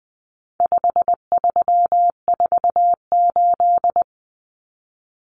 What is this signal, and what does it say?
Morse code, heard as a keyed tone, about 20 wpm, 700 Hz. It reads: 5348
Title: Common Cut Numbers - 4-Digit 20wpm